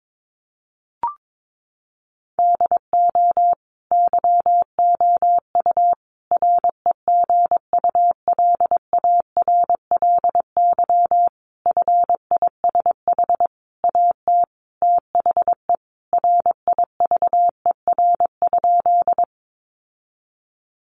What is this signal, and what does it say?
Morse code, heard as a keyed tone, about 22 wpm, 700 Hz. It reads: DO YOU REGULARLY FISH AT THE RIVER?
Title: Sentences from Top 1400 Words 22wpm